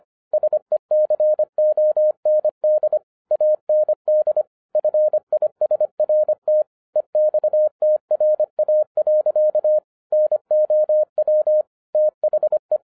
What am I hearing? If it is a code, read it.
Morse code, heard as a keyed tone, about 25 wpm, 610 Hz. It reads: SECOND AND FISRT EXTRA. NOW THE
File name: 2012-01-mp3 / n08-ITS MY FAULT, BUT-25wpm